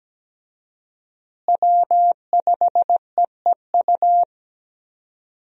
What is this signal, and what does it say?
Morse code, heard as a keyed tone, about 17 wpm, 700 Hz. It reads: W5EEU
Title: USA Call Signs - 5 Characters - Repeated 2x - 17wpm